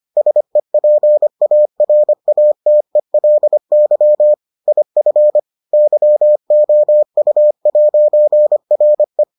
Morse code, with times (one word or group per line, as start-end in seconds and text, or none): 0.2-4.3 SEPARATELY
4.7-5.4 IF
5.7-9.2 YOU'RE